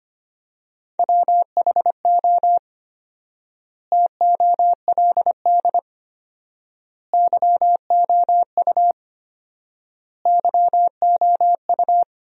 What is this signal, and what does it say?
Morse code, heard as a keyed tone, about 25 wpm, 700 Hz. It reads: WHO TOLD YOU YOU
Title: 4x Word-Spacing - Sentences from Top 400 Words - 25wpm